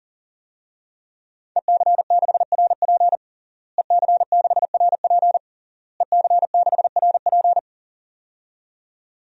Morse code, heard as a keyed tone, about 37 wpm, 700 Hz. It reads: EC6RP EC6RP EC6RP
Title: DXCC-100 Call Signs - 5 Characters - Repeated 3x - 40wpm